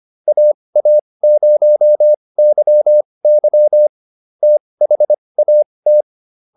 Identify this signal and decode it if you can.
Morse code, heard as a keyed tone, about 25 wpm, spaced as for 20 wpm, 600 Hz. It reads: AA0YY THAT